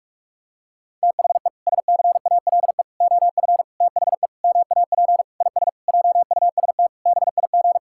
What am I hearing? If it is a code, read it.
Morse code, heard as a keyed tone, about 46 wpm, 700 Hz. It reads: THE SCALE OF THE MAP IS JUST BIG
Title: Sentences from Top 1000 Words - Review Part 5 - 45wpm